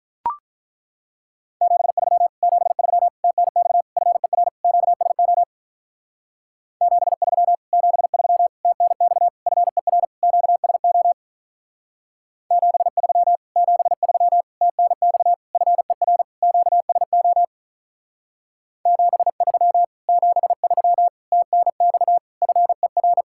Morse code, heard as a keyed tone, about 43 wpm, 700 Hz. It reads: 73 73 TNX FER QSO 73 73 TNX FER QSO 73 73 TNX FER QSO 73 73 TNX FER
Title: Speed-Racing Common QSO Phrases - 35wpm